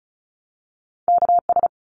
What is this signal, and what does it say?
Morse code, heard as a keyed tone, about 34 wpm, 700 Hz. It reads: KS
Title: ARRL Sections 35wpm